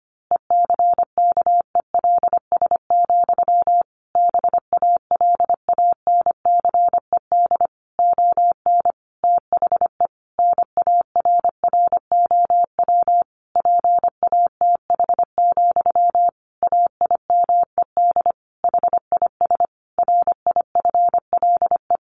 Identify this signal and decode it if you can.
Morse code, heard as a keyed tone, about 25 wpm, 700 Hz. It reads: ECKELS, BALANCED ON THE NARROW PATH, AIMED HIS RIFLE
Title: A Sound Of Thunder 25wpm